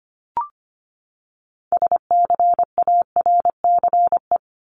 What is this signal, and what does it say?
Morse code, heard as a keyed tone, about 25 wpm, 700 Hz. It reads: SCARCE
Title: LY Suffix - Top 100 Words with Base Words 25wpm